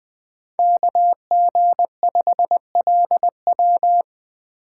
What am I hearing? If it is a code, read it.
Morse code, heard as a keyed tone, about 20 wpm, 700 Hz. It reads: KG5LW
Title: USA Call Signs - 5 Characters - 2x Character Spacing - 20wpm